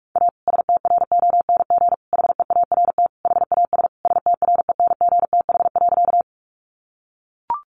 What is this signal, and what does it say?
Morse code, heard as a keyed tone, about 44 wpm, 700 Hz. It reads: A STRONG HEART HAS STRENGTH.
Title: Sentences from Top 1300 Words 45wpm